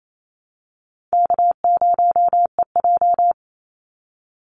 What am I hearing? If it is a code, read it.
Morse code, heard as a keyed tone, about 28 wpm, 700 Hz. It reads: K0EJ
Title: Call Signs Repeated 2x - 28wpm